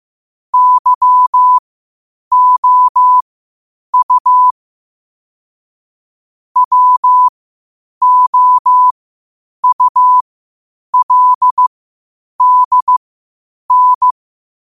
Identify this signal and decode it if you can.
Morse code, heard as a keyed tone, about 15 wpm, spaced as for 8 wpm, 1000 Hz. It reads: YOU WOULDN
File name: qotdfiles / 8wpm